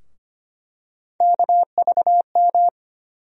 Morse code, heard as a keyed tone, about 25 wpm, 700 Hz. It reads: KVM
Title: USA Call Sign Postfixes - 3 Letters 25wpm